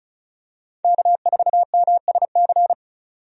Morse code, heard as a keyed tone, about 36 wpm, 700 Hz. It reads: K4MSC
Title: USA Call Signs - 5 Characters - 35wpm